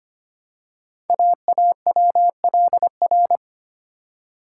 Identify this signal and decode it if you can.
Morse code, heard as a keyed tone, about 25 wpm, 700 Hz. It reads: AAWLR
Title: Code Groups - 5 Letters - 25wpm